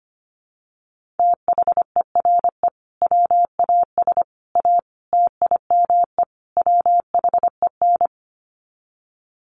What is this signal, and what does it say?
Morse code, heard as a keyed tone, about 25 wpm, 700 Hz. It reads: THERE WAS A TIME WHEN
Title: Sets of 5 Words 25wpm